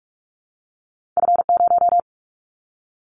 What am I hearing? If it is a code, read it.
Morse code, heard as a keyed tone, about 44 wpm, 700 Hz. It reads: F0
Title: Callsign Prefixes for Top 25 DXCC - 45wpm